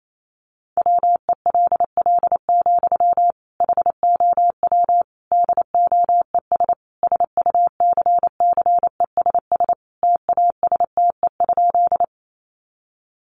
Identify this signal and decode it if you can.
Morse code, heard as a keyed tone, about 28 wpm, 700 Hz. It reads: WELL, HOW DOES SUCCESS TASTE?
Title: Sentences from Top 1100 Words 28wpm